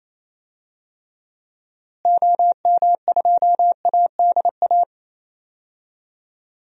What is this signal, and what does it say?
Morse code, heard as a keyed tone, about 28 wpm, 700 Hz. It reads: OM2ADA
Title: DXCC-100 Call Signs - 6 Characters - Repeated 3x - 28wpm